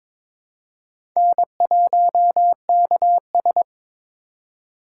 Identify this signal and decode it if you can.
Morse code, heard as a keyed tone, about 22 wpm, 700 Hz. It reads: N1KS
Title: USA Call Signs - 4 Characters - 6x Character Spacing - 22wpm